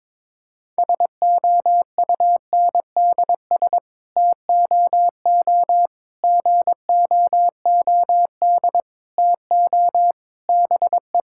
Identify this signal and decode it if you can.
Morse code, heard as a keyed tone, about 22 wpm, 700 Hz. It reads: SOUNDS TOO GOOD TO BE
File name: Sets of 9 Words - 22wpm